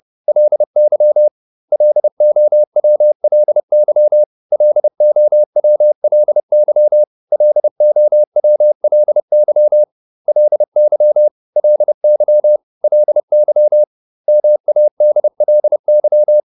Morse code, heard as a keyed tone, about 30 wpm, 600 Hz. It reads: LY LOWLY LOWLY LOWLY LY LY LY MADLY